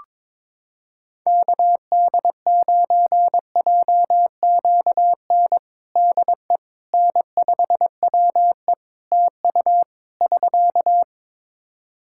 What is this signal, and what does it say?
Morse code, heard as a keyed tone, about 22 wpm, 700 Hz. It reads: KD9JQN DE N5WE TU <SK>
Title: Practice QSO - Exchange 4 - Ending - 22wpm